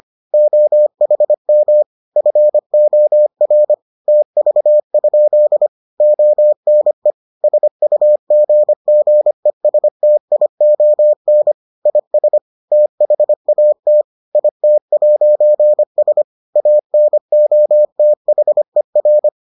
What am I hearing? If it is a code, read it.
Morse code, heard as a keyed tone, about 25 wpm, 600 Hz. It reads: OHM FOR TV? ONE SUGGESTION IS THAT IT'S ANOTHER